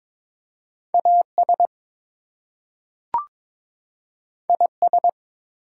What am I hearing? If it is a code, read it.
Morse code, heard as a keyed tone, about 22 wpm, 700 Hz. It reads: AS IS
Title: Two Letter Words - 22wpm